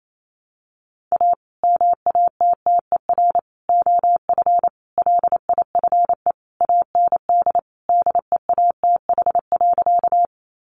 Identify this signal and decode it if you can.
Morse code, heard as a keyed tone, about 28 wpm, 700 Hz. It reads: A MATTER OF LIFE AND DEATH.
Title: Sentences from Top 1100 Words 28wpm